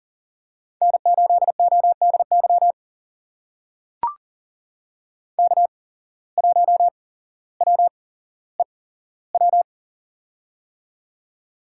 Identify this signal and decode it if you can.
Morse code, heard as a keyed tone, about 40 wpm, spaced as for 17 wpm, 700 Hz. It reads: N8ODY K1WEW
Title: USA Call Signs - 5 Characters - 8x Character Spacing - 40wpm